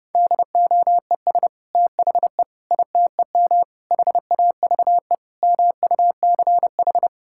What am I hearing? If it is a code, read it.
Morse code, heard as a keyed tone, about 32 wpm, 700 Hz. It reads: DOES THE ITEM HAVE MUCH
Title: Sentences from Top 1000 Words - Review Part 6 - 30wpm